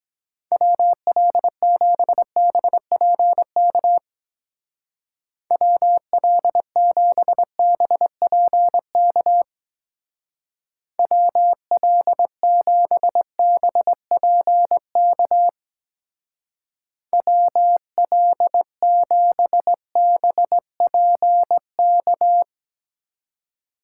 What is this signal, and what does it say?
Morse code, heard as a keyed tone, about 22 wpm, 700 Hz. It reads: WL7BPK WL7BPK WL7BPK WL7BPK